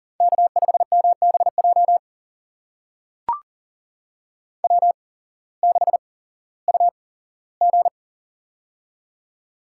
Morse code, heard as a keyed tone, about 40 wpm, spaced as for 19 wpm, 700 Hz. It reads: K5MBJ W6UG
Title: USA Call Signs - 8x Character Spacing - 40wpm